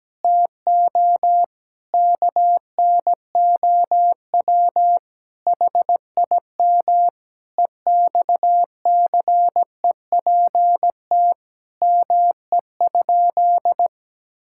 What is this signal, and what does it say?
Morse code, heard as a keyed tone, about 17 wpm, 700 Hz. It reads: TO KNOW HIM EXCEPT ME?
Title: Sentences from Top 1600 Words 17wpm